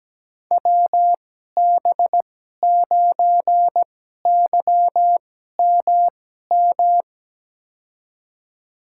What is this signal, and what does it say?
Morse code, heard as a keyed tone, about 17 wpm, spaced as for 12 wpm, 700 Hz. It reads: WB9YMM
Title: USA Call Signs - 6 Characters - 2x Character Spacing - 17wpm